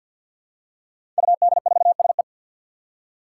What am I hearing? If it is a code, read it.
Morse code, heard as a keyed tone, about 53 wpm, 700 Hz. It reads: UD4SE
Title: Five Challenging Letter-Number Combinations 50wpm